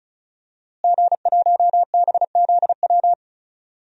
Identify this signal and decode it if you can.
Morse code, heard as a keyed tone, about 36 wpm, 700 Hz. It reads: G1BZW
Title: DXCC-100 Call Signs - 5 Characters - Repeated 3x - 35wpm